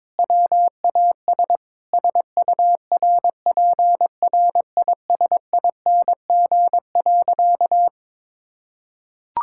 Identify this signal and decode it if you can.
Morse code, heard as a keyed tone, about 22 wpm, 700 Hz. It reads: WAS SURPRISING.
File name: practice / 22wpm / Sentences with 4-Word Phrases - 22wpm